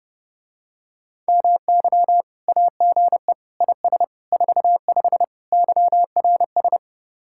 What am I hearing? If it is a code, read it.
Morse code, heard as a keyed tone, about 30 wpm, 700 Hz. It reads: MY AGE IS 45 YRS